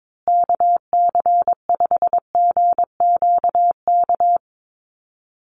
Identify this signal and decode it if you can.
Morse code, heard as a keyed tone, about 22 wpm, 700 Hz. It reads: KC5GQK
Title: USA Call Signs Repeated 2x 22wpm